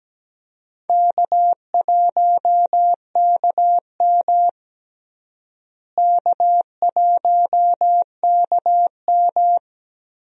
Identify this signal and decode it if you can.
Morse code, heard as a keyed tone, about 17 wpm, 700 Hz. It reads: K1KM K1KM
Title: USA Call Signs - 4 Characters - Repeated 2x - 17wpm